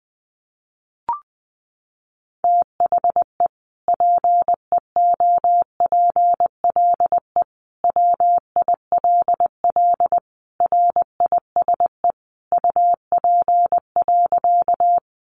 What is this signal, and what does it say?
Morse code, heard as a keyed tone, about 20 wpm, 700 Hz. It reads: THE PEOPLE WILL RISE UP.
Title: Sentences from Top 900 Words 20wpm